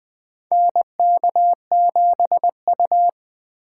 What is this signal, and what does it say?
Morse code, heard as a keyed tone, about 20 wpm, 700 Hz. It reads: NK7U
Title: K1USN SST QSO Practice - 20wpm